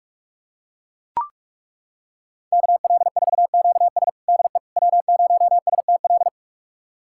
Morse code, heard as a keyed tone, about 42 wpm, 700 Hz. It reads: KL4QS DE W0STL
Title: USA Callsign DE Callsign - 5 Characters - 45wpm